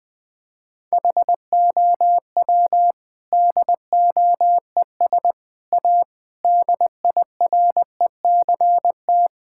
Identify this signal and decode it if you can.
Morse code, heard as a keyed tone, about 20 wpm, 700 Hz. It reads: HOW DOES A DIRECT
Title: Sentences from Top 1000 Words - Review Part 5 - 20wpm